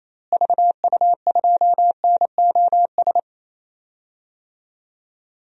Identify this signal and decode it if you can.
Morse code, heard as a keyed tone, about 28 wpm, 700 Hz. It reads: VU2NOS
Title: DXCC-100 Call Signs - 6 Characters - Repeated 2x - 28wpm